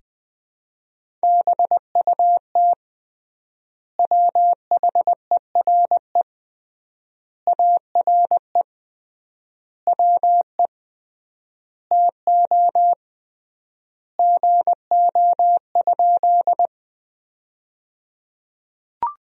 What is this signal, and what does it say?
Morse code, heard as a keyed tone, about 21 wpm, 700 Hz. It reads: BUT WHERE ARE WE TO GO?
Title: 3x Word-Spacing - Sentences from Top 200 Words - 20wpm